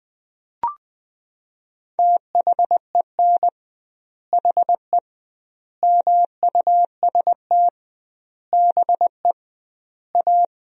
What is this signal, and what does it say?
Morse code, heard as a keyed tone, about 20 wpm, 700 Hz. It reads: THEN HE MUST BE A
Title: 2x Word-Spacing - Sentences from Top 500 Words - 20wpm